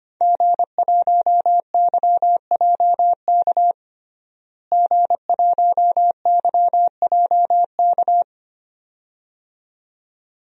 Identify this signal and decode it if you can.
Morse code, heard as a keyed tone, about 25 wpm, 700 Hz. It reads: G1YJK G1YJK
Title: DXCC-100 Call Signs - 5 Characters - Repeated 2x - 25wpm